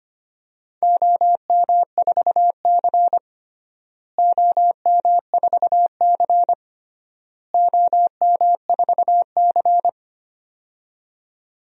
Morse code, heard as a keyed tone, about 25 wpm, 700 Hz. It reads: OM4C OM4C OM4C
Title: DXCC-100 Call Signs - 4 Characters - Repeated 3x - 25wpm